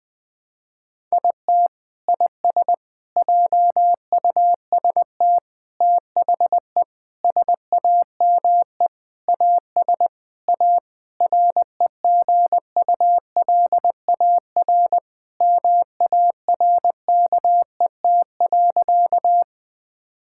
Morse code, heard as a keyed tone, about 20 wpm, 700 Hz. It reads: IT IS JUST THE SAME AS A REGULAR MARKET.
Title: Sentences from Top 1000 Words - Review Part 6 - 20wpm